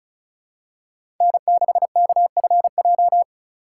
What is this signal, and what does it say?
Morse code, heard as a keyed tone, about 36 wpm, 700 Hz. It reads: N6KFJ
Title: USA Call Signs - 5 Characters - 3x Character Spacing - 35wpm